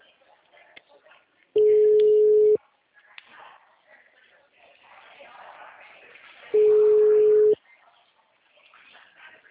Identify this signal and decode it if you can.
Ringback tone (Europe, 425 Hz).